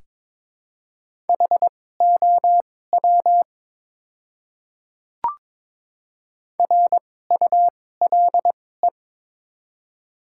Morse code, heard as a keyed tone, about 21 wpm, spaced as for 16 wpm, 700 Hz. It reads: HOW RULE
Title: 2x Character Spacing - Top 500 Words - 22wpm